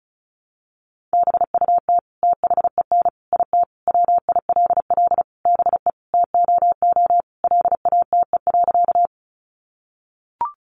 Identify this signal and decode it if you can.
Morse code, heard as a keyed tone, about 34 wpm, 700 Hz. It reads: BUT THEN IT WILL BE TOO LATE.
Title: Sentences from Top 300 Words 35wpm